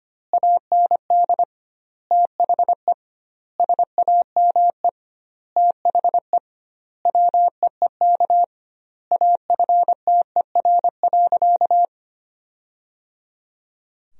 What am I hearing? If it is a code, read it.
Morse code, heard as a keyed tone, about 25 wpm, 700 Hz. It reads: AND THE SAME THE WEEK AFTER.
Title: 2x Word-Spacing - Sentences from Top 500 Words - 25wpm